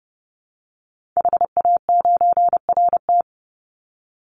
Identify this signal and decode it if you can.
Morse code, heard as a keyed tone, about 30 wpm, 700 Hz. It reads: HA9RT
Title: Call Signs Repeated 3x - 30wpm